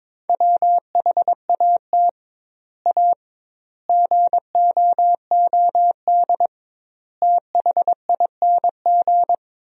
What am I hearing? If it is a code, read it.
Morse code, heard as a keyed tone, about 22 wpm, 700 Hz. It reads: WHAT A GOOD THING
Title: 2x Word-Spacing - Sentences from Top 200 Words - 22wpm